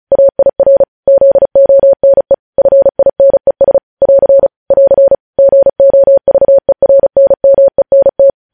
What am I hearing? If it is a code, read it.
Morse code, heard as a keyed tone, about 36 wpm, 550 Hz. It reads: AIR ZONE FINES <AR> <AR> GOVERNMENT
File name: podcast / News headlines-1x-35x35